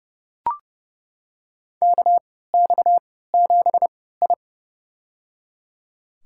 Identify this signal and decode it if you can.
Morse code, heard as a keyed tone, about 32 wpm, spaced as for 17 wpm, 700 Hz. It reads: KX7I
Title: USA Call Signs - 4 Characters - 3x Character Spacing - 30wpm